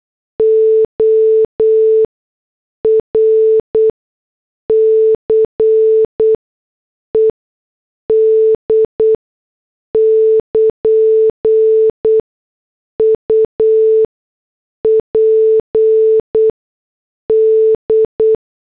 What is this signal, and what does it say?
Morse code, heard as a keyed tone, about 8 wpm, spaced as for 6 wpm, 440 Hz. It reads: ORCED <KN> UPD